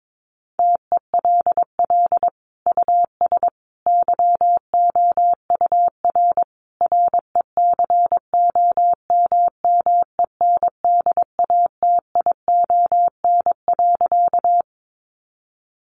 Morse code, heard as a keyed tone, about 22 wpm, 700 Hz. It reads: TELL US YOUR RECOMMENDATION.
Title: Sentences from Top 1400 Words 22wpm